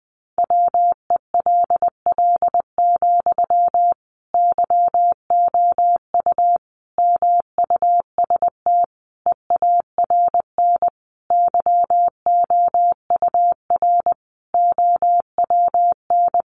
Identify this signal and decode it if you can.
Morse code, heard as a keyed tone, about 20 wpm, 700 Hz. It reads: WELL, YOU MUST EARN YOUR OWN
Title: Sentences from Top 1500 Words 20wpm